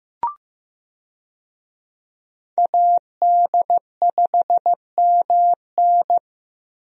Morse code, heard as a keyed tone, about 15 wpm, 700 Hz. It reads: AD5MN
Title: USA Call Signs - 5 Characters - Repeated 3x - 15wpm